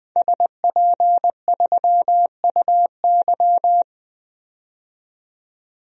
Morse code, heard as a keyed tone, about 20 wpm, 700 Hz. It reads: SP3UY
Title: DXCC-100 Call Signs - 5 Characters - 20wpm